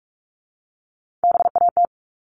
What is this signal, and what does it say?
Morse code, heard as a keyed tone, about 44 wpm, 700 Hz. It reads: BAT
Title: Top 1251-1500 Words 45wpm